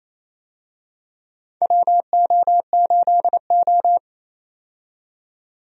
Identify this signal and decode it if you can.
Morse code, heard as a keyed tone, about 27 wpm, 700 Hz. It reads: WO8O
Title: USA Call Signs - 4 Characters - 28wpm